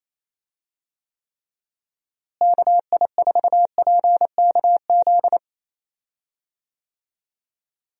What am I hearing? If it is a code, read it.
Morse code, heard as a keyed tone, about 28 wpm, 700 Hz. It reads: KI4PKZ
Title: USA Call Signs - 6 Characters - Repeated 3x - 28wpm